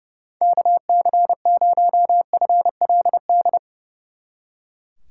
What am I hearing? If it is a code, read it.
Morse code, heard as a keyed tone, about 36 wpm, 700 Hz. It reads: KC0FLD
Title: Speed Racing - USA 2x3 Call Signs 30wpm